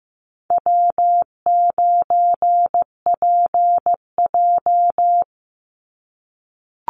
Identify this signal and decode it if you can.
Morse code, heard as a keyed tone, about 15 wpm, 700 Hz. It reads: W9PJ
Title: Easy Call Signs - 15wpm